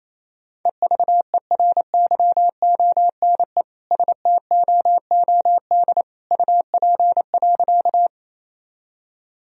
Morse code, heard as a keyed tone, about 28 wpm, 700 Hz. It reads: EVERYONE STOOD UP.
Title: Sentences from Top 1000 Words - Review Part 2 - 28wpm